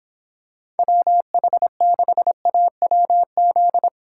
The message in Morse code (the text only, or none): WH6AWZ